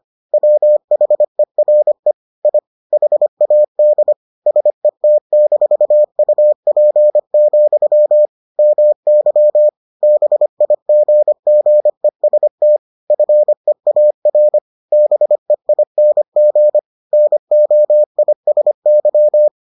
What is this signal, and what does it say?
Morse code, heard as a keyed tone, about 25 wpm, 600 Hz. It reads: WHERE I HAD SET-UP, MY BIGGEST FEAR BEING NOISY